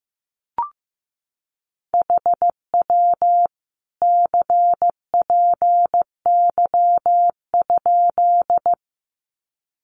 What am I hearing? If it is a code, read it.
Morse code, heard as a keyed tone, about 15 wpm, 700 Hz. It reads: HW CPY?
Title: Common QSO Phrases 15wpm